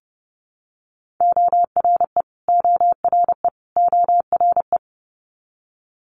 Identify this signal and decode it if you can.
Morse code, heard as a keyed tone, about 30 wpm, 700 Hz. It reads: ORE ORE ORE
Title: Multi-letter Phonemes Repeated 3x 30wpm